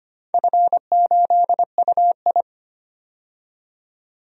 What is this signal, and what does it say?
Morse code, heard as a keyed tone, about 25 wpm, 700 Hz. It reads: F8UI
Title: DXCC-100 Call Signs - 4 Characters - 25wpm